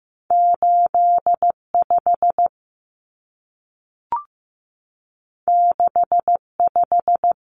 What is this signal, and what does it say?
Morse code, heard as a keyed tone, about 15 wpm, 700 Hz. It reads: 85 65
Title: Common Cut Numbers - 2-Digit 15wpm